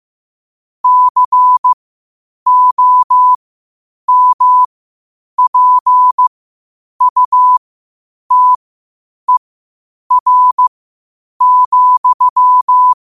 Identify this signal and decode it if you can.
Morse code, heard as a keyed tone, about 15 wpm, spaced as for 8 wpm, 1000 Hz. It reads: COMPUTER,
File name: qotdfiles / 8wpm